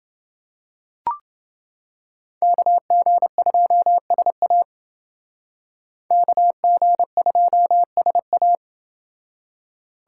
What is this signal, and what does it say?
Morse code, heard as a keyed tone, about 29 wpm, 700 Hz. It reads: KG2SA KG2SA